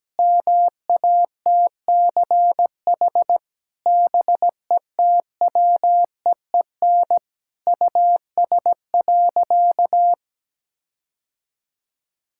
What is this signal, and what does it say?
Morse code, heard as a keyed tone, about 17 wpm, 700 Hz. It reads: MATCH BETWEEN US.
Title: Sentences from Top 1600 Words 17wpm